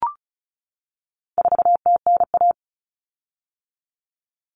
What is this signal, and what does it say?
Morse code, heard as a keyed tone, about 34 wpm, 700 Hz. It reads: 4TNA
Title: Common Cut Numbers - 4-Digit 35wpm